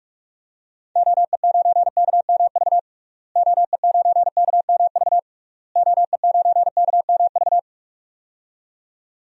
Morse code, heard as a keyed tone, about 46 wpm, 700 Hz. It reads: OE9KMV OE9KMV OE9KMV
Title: DXCC-100 Call Signs - 6 Characters - Repeated 3x - 45wpm